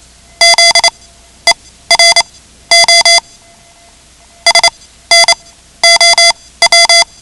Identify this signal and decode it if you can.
Morse code, heard as a keyed tone, about 28 wpm, spaced as for 14 wpm, 2050 Hz. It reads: ZERO SNOW
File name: ham radio / CodeQuick / Code Quick 10